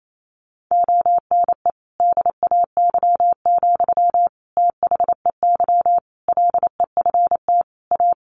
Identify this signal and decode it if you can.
Morse code, heard as a keyed tone, about 28 wpm, 700 Hz. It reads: ONE DAY, THEY LEFT A
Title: Sentences from Top 1300 Words 28wpm